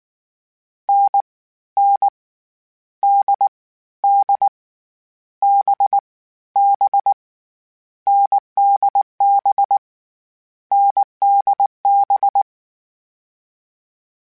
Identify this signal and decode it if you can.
Morse code, heard as a keyed tone, about 19 wpm, 800 Hz. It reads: N N D D B B NDB NDB
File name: Difficult Combos / Difficult Combos-mp3s / awj1 ndb6 gz7 @18